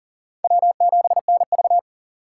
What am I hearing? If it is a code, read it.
Morse code, heard as a keyed tone, about 40 wpm, 700 Hz. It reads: W7NV
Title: USA Call Signs - 4 Characters - 2x Character Spacing - 40wpm